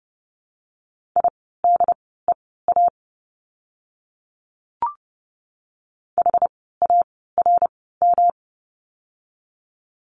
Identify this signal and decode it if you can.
Morse code, heard as a keyed tone, about 30 wpm, spaced as for 17 wpm, 700 Hz. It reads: IDEA HARM
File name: course / 30:10wpm / 108 Lesson 18f - Letter m - Words 30wpm